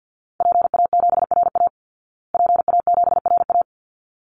Morse code, heard as a keyed tone, about 50 wpm, 700 Hz. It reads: PA7RA PA7RA
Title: Call Signs Repeated 3x - 50wpm